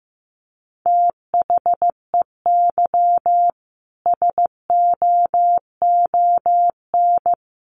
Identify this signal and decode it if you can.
Morse code, heard as a keyed tone, about 14 wpm, 700 Hz. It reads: THEY SOON
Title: Sentences from Top 1300 Words 15wpm